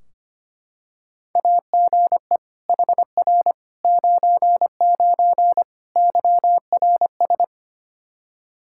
Morse code, heard as a keyed tone, about 25 wpm, 700 Hz. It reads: AGE HR 99 YRS
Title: Practice QSO - Exchange 3 - More Info - 25wpm